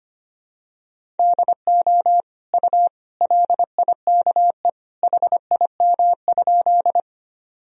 Morse code, heard as a keyed tone, about 25 wpm, 700 Hz. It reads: DO U LIKE HIM?
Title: Sentences from Top 100 Words and Abbreviations 25wpm